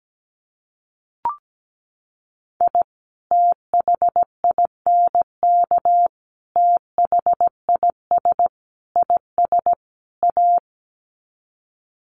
Morse code, heard as a keyed tone, about 17 wpm, 700 Hz. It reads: I THINK THIS IS A
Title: Sets of 5 Words 17wpm